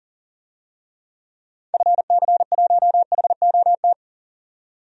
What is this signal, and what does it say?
Morse code, heard as a keyed tone, about 39 wpm, 700 Hz. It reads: FC1HOT
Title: DXCC-100 Call Signs - 6 Characters - 40wpm